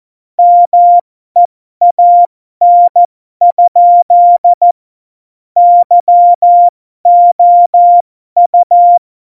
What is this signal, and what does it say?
Morse code, heard as a keyed tone, about 14 wpm, spaced as for 12 wpm, 700 Hz. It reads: MEAN? YOU
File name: Morsekurs / Witze